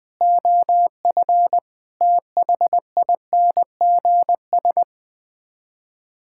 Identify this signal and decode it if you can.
Morse code, heard as a keyed tone, about 20 wpm, 700 Hz. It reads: OF THINGS
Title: Sets of 6 Words - 20wpm